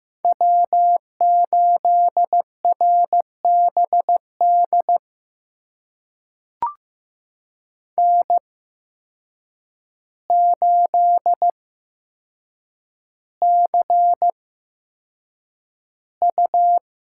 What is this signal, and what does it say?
Morse code, heard as a keyed tone, about 15 wpm, spaced as for 7 wpm, 700 Hz. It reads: W8RBD N8CU
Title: USA Call Signs - 5 Characters - 8x Character Spacing - 15wpm